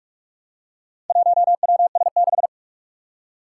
Morse code, heard as a keyed tone, about 41 wpm, 700 Hz. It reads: 1WS6
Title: Four Challenging Letter-Number Combinations 45wpm